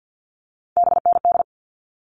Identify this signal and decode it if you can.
Morse code, heard as a keyed tone, about 50 wpm, 700 Hz. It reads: BND
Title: All Cut Numbers - 3-Digit 50wpm